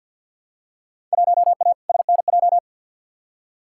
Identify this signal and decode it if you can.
Morse code, heard as a keyed tone, about 54 wpm, 700 Hz. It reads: 1A SNJ
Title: Field Day Exchanges - 50wpm